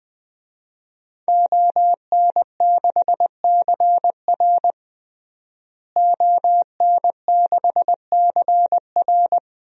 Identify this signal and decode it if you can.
Morse code, heard as a keyed tone, about 20 wpm, 700 Hz. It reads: ON6CR ON6CR
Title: DXCC-100 Call Signs - 5 Characters - Repeated 2x - 20wpm